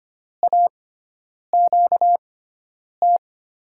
Morse code, heard as a keyed tone, about 25 wpm, spaced as for 9 wpm, 700 Hz. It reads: AQT
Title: USA Call Signs - 5 Characters - 6x Character Spacing - 25wpm